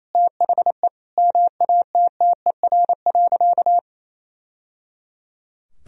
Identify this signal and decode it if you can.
Morse code, heard as a keyed tone, about 28 wpm, 700 Hz. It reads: THE MATTER.
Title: Sentences from Top 1000 Words - Review Part 3 - 28wpm